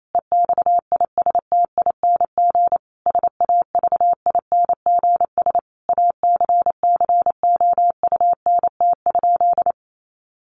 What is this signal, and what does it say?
Morse code, heard as a keyed tone, about 28 wpm, 700 Hz. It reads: EXISTING SAVINGS ACCOUNT?
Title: Sentences from Top 1500 Words 28wpm